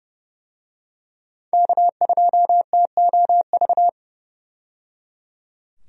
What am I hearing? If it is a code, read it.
Morse code, heard as a keyed tone, about 30 wpm, 700 Hz. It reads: K2TOV
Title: USA Call Signs - 5 Characters - 30wpm